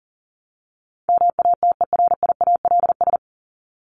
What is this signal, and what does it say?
Morse code, heard as a keyed tone, about 40 wpm, 700 Hz. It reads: MATERIALS
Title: ES:S Suffix - Top 100 Words with Base Words 40wpm